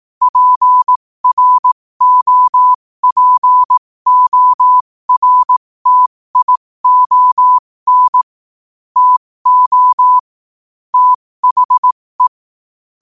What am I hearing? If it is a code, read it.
Morse code, heard as a keyed tone, about 18 wpm, spaced as for 15 wpm, 1000 Hz. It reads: PROPORTION TO THE
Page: New quotes every day in morse code at 15 Words per minute.